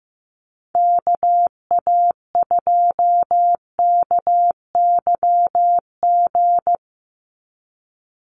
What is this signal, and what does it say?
Morse code, heard as a keyed tone, about 15 wpm, 700 Hz. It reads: KA2KYG
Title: USA Call Signs 15wpm